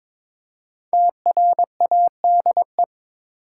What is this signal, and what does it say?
Morse code, heard as a keyed tone, about 22 wpm, 700 Hz. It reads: TRADE
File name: 5 Letter Words - First 3 Letters Given - 22wpm